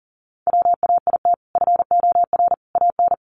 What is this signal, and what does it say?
Morse code, heard as a keyed tone, about 40 wpm, 700 Hz. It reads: WAIT FOR AN
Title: Sentences from Top 1000 Words 40wpm